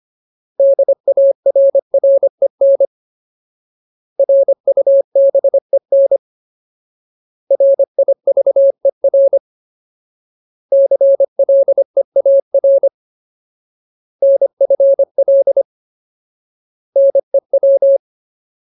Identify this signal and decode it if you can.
Morse code, heard as a keyed tone, about 25 wpm, 550 Hz. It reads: DARREN RUBEN RIVER CLEAR NFL NEW
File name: Ultimate Word List / 25WPM Ultimate